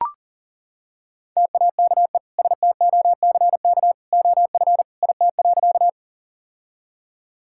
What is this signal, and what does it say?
Morse code, heard as a keyed tone, about 40 wpm, 700 Hz. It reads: TAKE STOCK OF IT.
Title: Sentences from Top 1000 Words - Review Part 5 - 40wpm